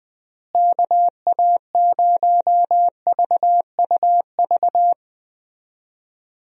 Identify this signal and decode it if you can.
Morse code, heard as a keyed tone, about 21 wpm, 700 Hz. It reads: KA0VUV
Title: USA Call Signs - 6 Characters - Repeated 2x - 20wpm